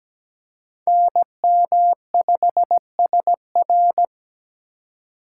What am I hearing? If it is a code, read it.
Morse code, heard as a keyed tone, about 20 wpm, 700 Hz. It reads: NM5SR
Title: Speed Racing - USA Call Signs 17wpm